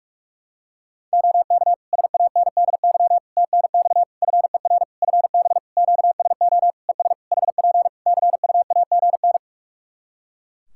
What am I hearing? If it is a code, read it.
Morse code, heard as a keyed tone, about 46 wpm, 700 Hz. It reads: OK SANDY TNX FER FB QSO ES HP CUAGN
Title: K1USN SST QSO Practice 45wpm